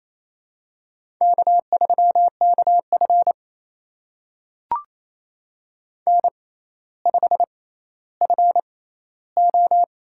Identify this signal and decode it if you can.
Morse code, heard as a keyed tone, about 28 wpm, spaced as for 14 wpm, 700 Hz. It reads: K3KF N5FO
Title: USA Call Signs - 6x Character Spacing - 28wpm